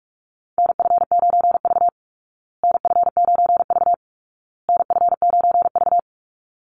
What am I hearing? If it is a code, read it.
Morse code, heard as a keyed tone, about 44 wpm, 700 Hz. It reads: NF9V NF9V NF9V
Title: Call Signs Repeated 2x - 45wpm